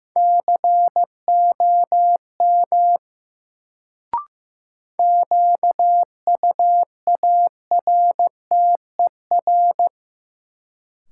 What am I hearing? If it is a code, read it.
Morse code, heard as a keyed tone, about 15 wpm, 700 Hz. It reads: COM QUARTER
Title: Sentences from Top 1300 Words - New Word Drill - 15wpm